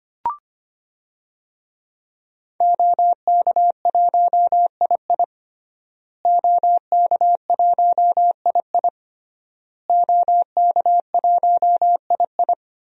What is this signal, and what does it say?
Morse code, heard as a keyed tone, about 25 wpm, 700 Hz. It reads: OK1II OK1II OK1II
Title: DXCC-100 Call Signs - 5 Characters - Repeated 3x - 25wpm